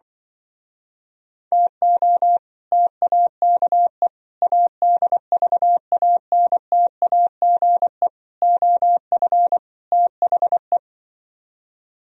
Morse code, heard as a keyed tone, about 25 wpm, 700 Hz. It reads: TO TAKE ADVANTAGE OF THE
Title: Speed-Racing - Sets of 5 Words 20wpm